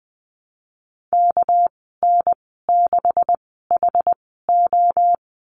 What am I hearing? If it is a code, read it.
Morse code, heard as a keyed tone, about 20 wpm, spaced as for 15 wpm, 700 Hz. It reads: KN6HO
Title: 213 Lesson 34g - Number 8 - Call Signs 20wpm